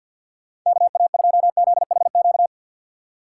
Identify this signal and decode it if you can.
Morse code, heard as a keyed tone, about 49 wpm, 700 Hz. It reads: KA2ZHQ
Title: USA Call Signs - 6 Characters - 6x Character Spacing - 50wpm